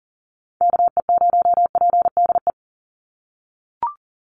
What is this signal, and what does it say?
Morse code, heard as a keyed tone, about 40 wpm, 700 Hz. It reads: KE0PDE
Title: USA Call Signs 40wpm